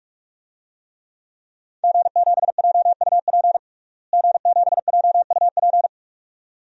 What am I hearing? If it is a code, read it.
Morse code, heard as a keyed tone, about 47 wpm, 700 Hz. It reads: G7JUP G7JUP
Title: DXCC-100 Call Signs - 5 Characters - Repeated 2x - 45wpm